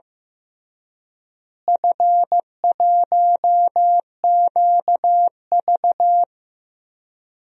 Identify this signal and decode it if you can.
Morse code, heard as a keyed tone, about 15 wpm, 700 Hz. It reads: F1QV
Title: DXCC-100 Call Signs - 4 Characters - Repeated 3x - 15wpm